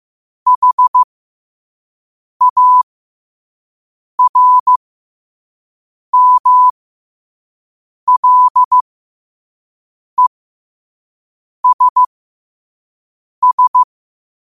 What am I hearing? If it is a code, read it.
Morse code, heard as a keyed tone, about 15 wpm, spaced as for 5 wpm, 1000 Hz. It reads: HARMLESS
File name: 5wpm